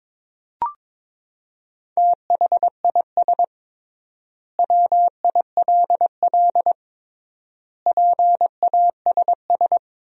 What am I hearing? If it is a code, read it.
Morse code, heard as a keyed tone, about 22 wpm, 700 Hz. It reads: THIS WILL PASS
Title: 3x Word-Spacing - Sentences from Top 400 Words - 22wpm